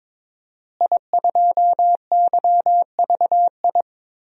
Morse code, heard as a keyed tone, about 22 wpm, 700 Hz. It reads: I2YVI
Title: DXCC-100 Call Signs - 5 Characters - Repeated 3x - 22wpm